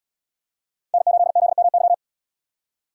Morse code, heard as a keyed tone, about 74 wpm, 700 Hz. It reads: N8ZMY